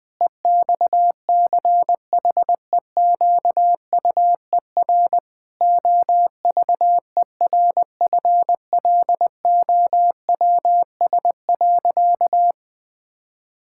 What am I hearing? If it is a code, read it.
Morse code, heard as a keyed tone, about 20 wpm, 700 Hz. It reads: EXCHEQUER OVERFLOWS.
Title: Pangram Challenge v2 - 20wpm